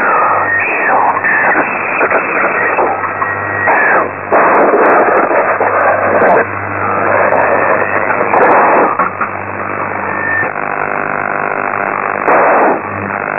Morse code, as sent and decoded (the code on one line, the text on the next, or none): none